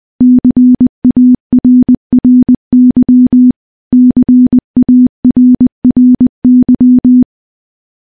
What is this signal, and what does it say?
Morse code, heard as a keyed tone, about 21 wpm, 250 Hz. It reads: CARRY CARRY
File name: morse-training-20WPM